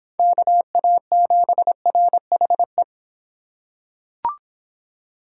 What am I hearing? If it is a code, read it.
Morse code, heard as a keyed tone, about 26 wpm, 700 Hz. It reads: KA7RHE